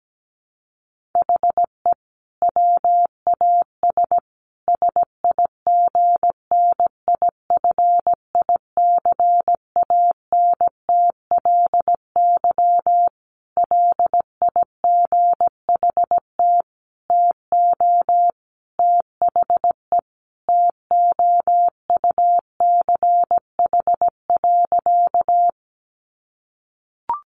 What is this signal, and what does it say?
Morse code, heard as a keyed tone, about 17 wpm, 700 Hz. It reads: HE WAS SIGNIFICANTLY LIGHT TO THE TOUCH.
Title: Sentences from Top 1500 Words 17wpm